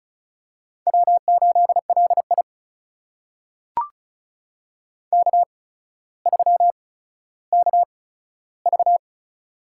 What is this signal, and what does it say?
Morse code, heard as a keyed tone, about 36 wpm, spaced as for 15 wpm, 700 Hz. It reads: W8LI K3KV
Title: USA Call Signs - 8x Character Spacing - 35wpm